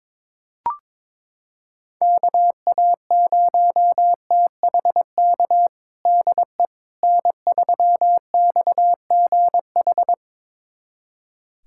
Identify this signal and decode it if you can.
Morse code, heard as a keyed tone, about 21 wpm, 700 Hz. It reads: KA0THK DE N3XGH
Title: USA Callsign DE Callsign 22wpm